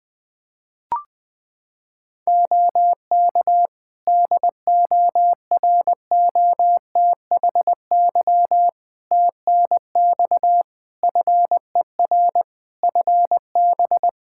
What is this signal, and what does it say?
Morse code, heard as a keyed tone, about 20 wpm, 700 Hz. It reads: OK DOROTHY TNX FER FB
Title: Practice QSO - Exchange 4 - Ending - 20wpm